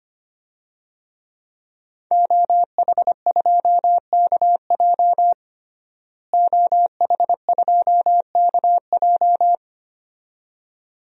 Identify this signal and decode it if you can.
Morse code, heard as a keyed tone, about 25 wpm, 700 Hz. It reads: OH2KJ OH2KJ
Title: DXCC-100 Call Signs - 5 Characters - Repeated 2x - 25wpm